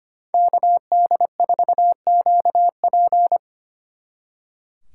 Morse code, heard as a keyed tone, about 29 wpm, 700 Hz. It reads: KD4QP
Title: Speed Racing - USA 2x2 Call Signs 25wpm